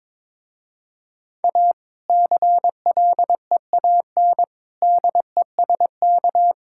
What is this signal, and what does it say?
Morse code, heard as a keyed tone, about 22 wpm, 700 Hz. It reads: A CLEAN DESK
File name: Difficult Sentences with Hints - 22wpm